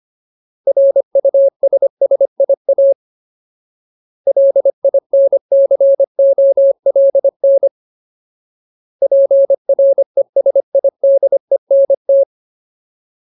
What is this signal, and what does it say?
Morse code, heard as a keyed tone, about 25 wpm, 550 Hz. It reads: RUSSIA LINCOLN PRESIDENT
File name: Ultimate Word List / 25WPM Ultimate